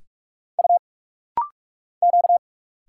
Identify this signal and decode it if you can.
Morse code, heard as a keyed tone, about 46 wpm, 700 Hz. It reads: U Q
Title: Single Letters - Rapid-Fire 45wpm